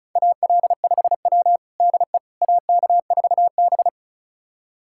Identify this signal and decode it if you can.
Morse code, heard as a keyed tone, about 36 wpm, 700 Hz. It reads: AL5W DE AK4B
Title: USA Callsign DE Callsign - 4 Characters - 35wpm